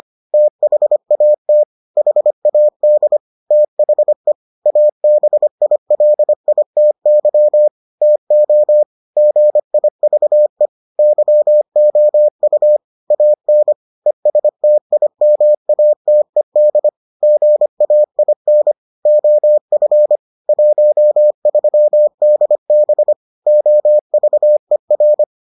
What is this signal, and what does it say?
Morse code, heard as a keyed tone, about 25 wpm, 600 Hz. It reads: THAT HAD THE ABILITY TO GIVE YOU AN ESTIMATED GAIN OF 13DB OVER